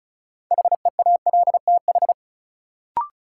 Morse code, heard as a keyed tone, about 34 wpm, 700 Hz. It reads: HEALTH
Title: Sets of 5 Words - New Word Drill - 35wpm